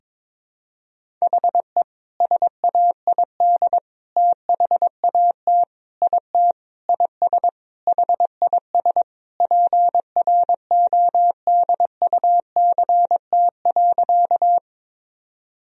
Morse code, heard as a keyed tone, about 22 wpm, 700 Hz. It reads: HE SAID THAT IT IS HIS PRODUCT.
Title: Sentences from Top 1000 Words - Review Part 2 - 22wpm